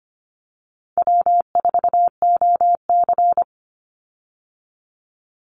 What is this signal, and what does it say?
Morse code, heard as a keyed tone, about 25 wpm, 700 Hz. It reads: W4OC
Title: Easy Call Signs Repeated 3x - 25wpm